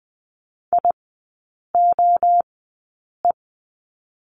Morse code, heard as a keyed tone, about 20 wpm, 700 Hz. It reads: I O E
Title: 029 Lesson 6e - Letter i - Three Characters 20wpm